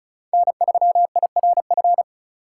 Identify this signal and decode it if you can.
Morse code, heard as a keyed tone, about 34 wpm, 700 Hz. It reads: N3IRF
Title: USA Call Signs - 4x Character Spacing - 35wpm